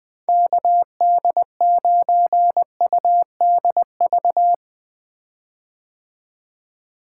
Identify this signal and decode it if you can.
Morse code, heard as a keyed tone, about 20 wpm, 700 Hz. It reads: KD9UDV
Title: 2023 POTA Activator Call Signs - 20wpm